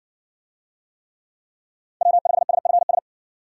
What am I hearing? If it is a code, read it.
Morse code, heard as a keyed tone, about 57 wpm, 700 Hz. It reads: W5SFS